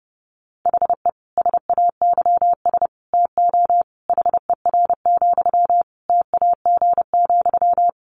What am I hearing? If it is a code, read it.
Morse code, heard as a keyed tone, about 30 wpm, 700 Hz. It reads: HE SAYS TO HER, TAG,
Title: Additional Easy Phrases 30wpm